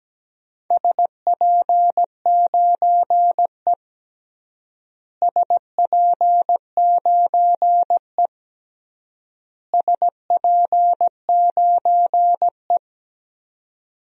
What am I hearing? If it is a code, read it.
Morse code, heard as a keyed tone, about 17 wpm, 700 Hz. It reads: SP9E SP9E SP9E
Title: DXCC-100 Call Signs - 4 Characters - Repeated 3x - 17wpm